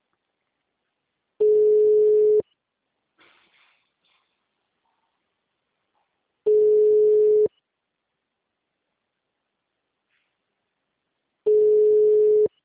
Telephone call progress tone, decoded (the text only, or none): ringback tone